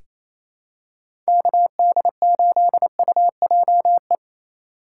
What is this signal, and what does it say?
Morse code, heard as a keyed tone, about 28 wpm, 700 Hz. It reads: KD8UJE
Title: USA Call Signs - 4x Character Spacing - 28wpm